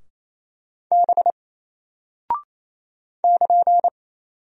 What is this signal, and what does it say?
Morse code, heard as a keyed tone, about 27 wpm, 700 Hz. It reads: B <KN>
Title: Single Character (Letter:Number:Punctuation:Prosign) 28wpm